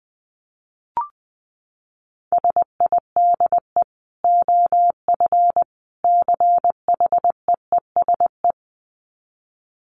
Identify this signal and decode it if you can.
Morse code, heard as a keyed tone, about 20 wpm, 700 Hz. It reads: SIDE OF CHEESE
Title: Dit Challenge 20wpm